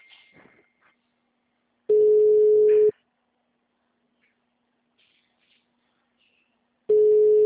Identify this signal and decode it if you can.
Ringback tone (Europe, 425 Hz).
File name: out- -2006- - - .97927